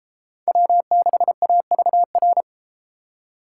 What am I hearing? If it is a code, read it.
Morse code, heard as a keyed tone, about 30 wpm, 700 Hz. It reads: W6AVR